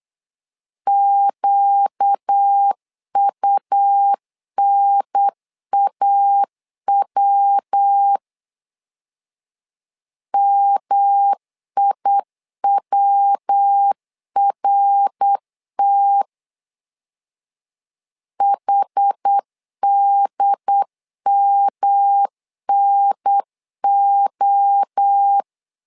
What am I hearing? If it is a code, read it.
Morse code, heard as a keyed tone, about 8 wpm, 800 Hz. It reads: QUNAW MIWRT HDMNO
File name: 4 Q 8wpm